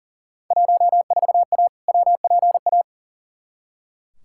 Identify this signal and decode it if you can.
Morse code, heard as a keyed tone, about 37 wpm, 700 Hz. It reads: 14A WPA
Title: Field Day Exchanges Encore - 40wpm